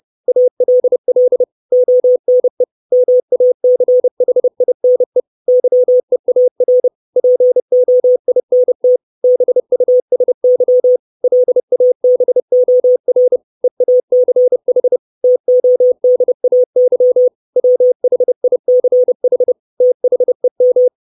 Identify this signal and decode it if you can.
Morse code, heard as a keyed tone, about 30 wpm, 500 Hz. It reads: ALL ONE MACHINE YEAR POINT BUSY LABOR EACH TODAY WHICH THEM
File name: 632 words / 632 Words 30WPM